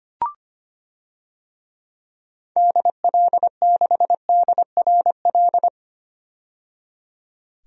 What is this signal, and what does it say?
Morse code, heard as a keyed tone, about 25 wpm, 700 Hz. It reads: DL6DRL
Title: DXCC-100 Call Signs - 6 Characters - 25wpm